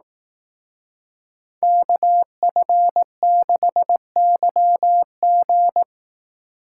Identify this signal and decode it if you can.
Morse code, heard as a keyed tone, about 18 wpm, 700 Hz. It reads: KF6YG
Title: Speed Racing - USA 2x2 Call Signs 15wpm